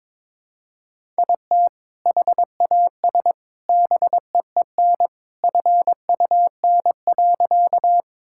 Morse code, heard as a keyed tone, about 22 wpm, 700 Hz. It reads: IT HAS BEEN FUN.
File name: practice / 22wpm / Sentences with 2-Word Phrases - 22wpm